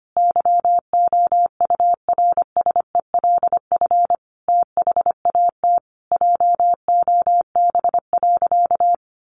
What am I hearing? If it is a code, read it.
Morse code, heard as a keyed tone, about 25 wpm, 700 Hz. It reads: YOURSELF THAT JOB.
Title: Sentences from Top 1100 Words 25wpm